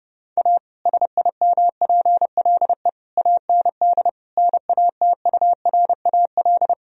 Morse code, heard as a keyed tone, about 32 wpm, 700 Hz. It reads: A SIMPLE AND NATURAL
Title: Sentences from Top 1000 Words - Review Part 4 - 30wpm